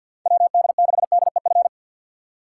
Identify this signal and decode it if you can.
Morse code, heard as a keyed tone, about 53 wpm, 700 Hz. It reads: WD6DEF
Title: USA Call Signs - 6 Characters - 8x Character Spacing - 50wpm